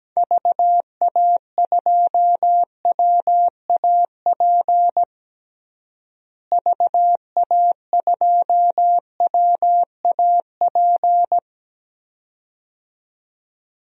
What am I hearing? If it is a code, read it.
Morse code, heard as a keyed tone, about 17 wpm, 700 Hz. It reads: VA2WAP VA2WAP
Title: DXCC-100 Call Signs - 6 Characters - Repeated 2x - 17wpm